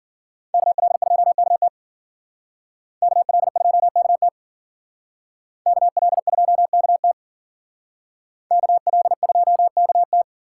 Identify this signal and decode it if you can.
Morse code, heard as a keyed tone, about 51 wpm, 700 Hz. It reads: KL2KT KL2KT KL2KT KL2KT